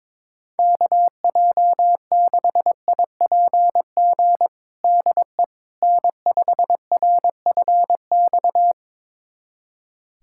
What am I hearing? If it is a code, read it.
Morse code, heard as a keyed tone, about 22 wpm, 700 Hz. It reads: KJ6IPG DE N5RFX
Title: USA Callsign DE Callsign 22wpm